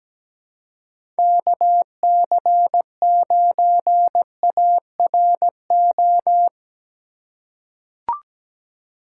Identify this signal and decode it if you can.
Morse code, heard as a keyed tone, about 17 wpm, 700 Hz. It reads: KC9ARO
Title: USA Call Signs - 2x Character Spacing - 17wpm